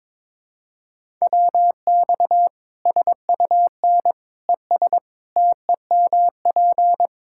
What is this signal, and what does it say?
Morse code, heard as a keyed tone, about 22 wpm, 700 Hz. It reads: WX SUN ES TEMP
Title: Practice QSO - Exchange 2 - Info - 22wpm